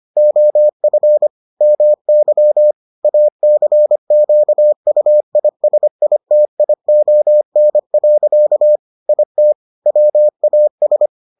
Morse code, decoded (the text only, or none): OF MY ACQUISITION. IT WAS